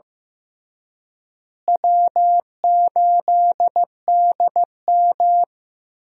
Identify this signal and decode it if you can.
Morse code, heard as a keyed tone, about 15 wpm, 700 Hz. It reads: W8DM
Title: USA Call Signs - 4 Characters - Repeated 3x - 15wpm